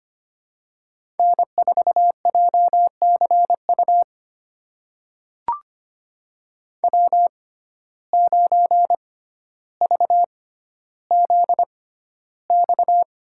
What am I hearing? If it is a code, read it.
Morse code, heard as a keyed tone, about 25 wpm, spaced as for 13 wpm, 700 Hz. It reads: N4JCU W9VZX
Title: USA Call Signs - 5 Characters - 6x Character Spacing - 25wpm